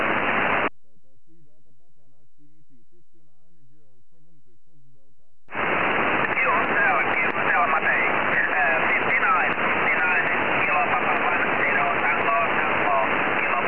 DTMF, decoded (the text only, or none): none